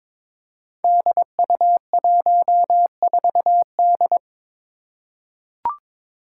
Morse code, heard as a keyed tone, about 21 wpm, 700 Hz. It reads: DU14D
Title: Five Challenging Letter-Number Combinations 22wpm